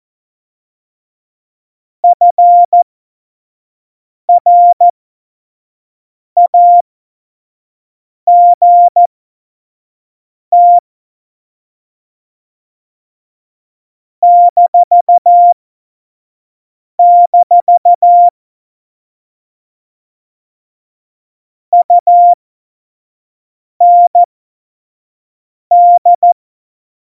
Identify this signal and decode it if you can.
Morse code, heard as a keyed tone, about 14 wpm, spaced as for 5 wpm, 700 Hz. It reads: FRAGT -- UND
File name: Morsekurs / Witze / Witz 10 April 2018, 25 F70